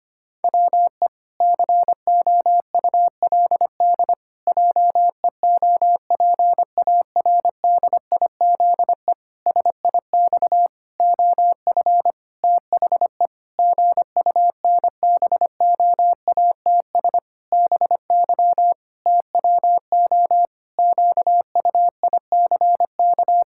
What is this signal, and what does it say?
Morse code, heard as a keyed tone, about 25 wpm, 700 Hz. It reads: WE COULD JEOPARDIZE SIX OF THE GUNBOATS BY TWO QUICK
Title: Pangram Challenge v2 - 25wpm